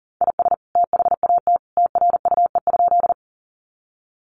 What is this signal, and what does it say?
Morse code, heard as a keyed tone, about 40 wpm, 700 Hz. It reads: IS THAT TRUE?
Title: Sentences from Top 400 Words 40wpm